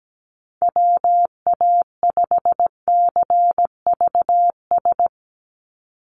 Morse code, heard as a keyed tone, about 17 wpm, 700 Hz. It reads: WA5CVS
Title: USA Call Signs 17wpm